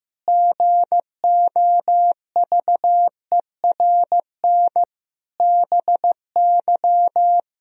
Morse code, heard as a keyed tone, about 15 wpm, 700 Hz. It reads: GOVERN BY
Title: Sentences from Top 1000 Words - Review Part 2 - 15wpm